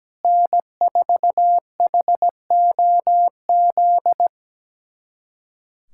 Morse code, heard as a keyed tone, about 17 wpm, 700 Hz. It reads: N4HOZ
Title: USA Callsign DE Callsign - 5 Characters - 17wpm